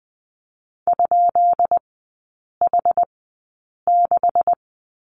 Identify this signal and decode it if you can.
Morse code, heard as a keyed tone, about 20 wpm, spaced as for 8 wpm, 700 Hz. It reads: ?H6
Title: 205 Lesson 33e - Letter j - Three Characters 20wpm